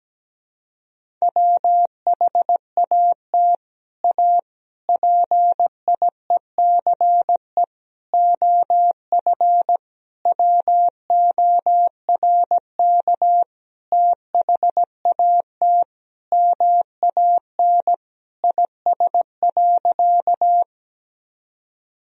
Morse code, heard as a keyed tone, about 17 wpm, 700 Hz. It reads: WHAT A PIECE OF WORK THAT MAN IS.
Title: Sentences from Top 1000 Words - Review Part 2 - 17wpm